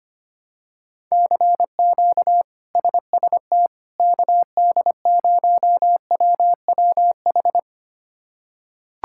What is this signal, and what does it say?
Morse code, heard as a keyed tone, about 25 wpm, 700 Hz. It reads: CQ SST KD0WWH
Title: K1USN SST QSO Practice - 25wpm